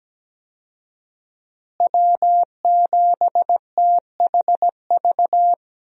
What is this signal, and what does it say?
Morse code, heard as a keyed tone, about 17 wpm, 700 Hz. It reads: W7THV
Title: USA Call Signs - 5 Characters - Repeated 2x - 17wpm